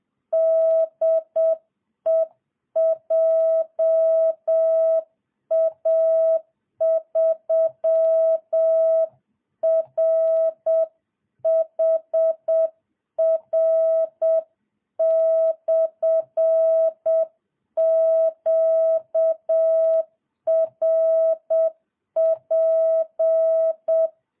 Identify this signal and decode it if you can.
Morse code, heard as a keyed tone, about 7 wpm, 660 Hz. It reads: DE JA3RHR/QRP